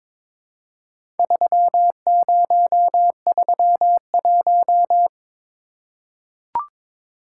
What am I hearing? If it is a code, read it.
Morse code, heard as a keyed tone, about 22 wpm, 700 Hz. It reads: 3031
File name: practice / 22wpm / Four Letter-Number Combinations - 22wpm